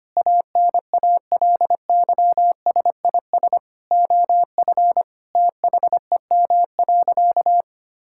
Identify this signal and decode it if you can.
Morse code, heard as a keyed tone, about 25 wpm, 700 Hz. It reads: ANALYSIS OF THEM.
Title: Sentences from Top 1000 Words - Review Part 5 - 25wpm